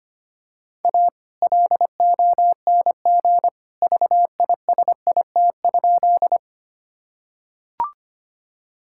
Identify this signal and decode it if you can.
Morse code, heard as a keyed tone, about 25 wpm, 700 Hz. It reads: A LONG VISIT?
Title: Sentences from Top 1000 Words - Review Part 6 - 25wpm